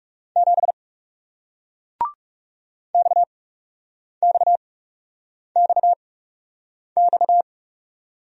Morse code, heard as a keyed tone, about 39 wpm, spaced as for 8 wpm, 700 Hz. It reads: 7 XXXX